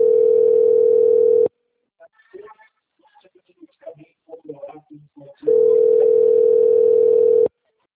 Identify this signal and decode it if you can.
Ringback tone (North America), 440 + 480 Hz, with speech.
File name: out- - - - - .72115